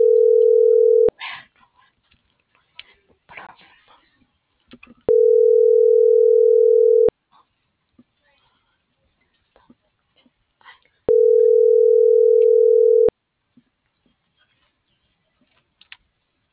Ringback tone (North America), 440 + 480 Hz.